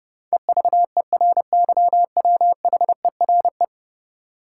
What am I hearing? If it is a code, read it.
Morse code, heard as a keyed tone, about 32 wpm, 700 Hz. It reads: EVERYWHERE
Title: Sentences from Top 1600 Words - New Word Drill - 30wpm